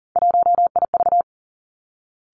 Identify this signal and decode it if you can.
Morse code, heard as a keyed tone, about 42 wpm, 700 Hz. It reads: 1IV
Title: Three Challenging Letter-Number Combinations 40wpm